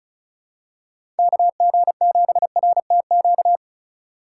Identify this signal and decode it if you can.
Morse code, heard as a keyed tone, about 34 wpm, 700 Hz. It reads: KG7RTQ
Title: USA Call Signs - 6 Characters - 2x Character Spacing - 35wpm